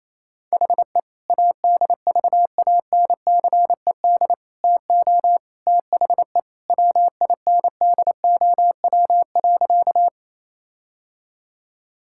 Morse code, heard as a keyed tone, about 27 wpm, 700 Hz. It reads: HE ADVANCED TO THE WINDOW.
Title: Sentences from Top 1600 Words 28wpm